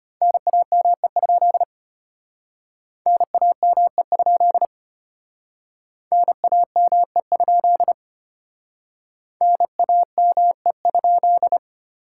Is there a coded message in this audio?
Yes: Morse code from 0.2 to 11.6 s, heard as a keyed tone, about 32 wpm, 700 Hz. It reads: NAME? NAME? NAME? NAME?